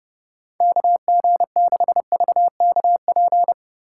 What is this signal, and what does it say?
Morse code, heard as a keyed tone, about 30 wpm, 700 Hz. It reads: KG6VKP
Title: USA Call Signs - 6 Characters - 3x Character Spacing - 30wpm